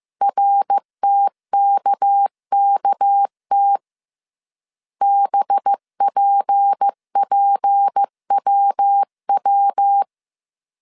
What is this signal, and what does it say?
Morse code, heard as a keyed tone, about 15 wpm, 800 Hz. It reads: RTKKT BPPWW